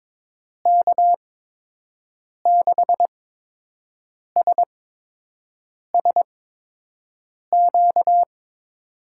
Morse code, heard as a keyed tone, about 22 wpm, spaced as for 6 wpm, 700 Hz. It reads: K6SSQ
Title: USA Call Signs - 6 Characters - 8x Character Spacing - 22wpm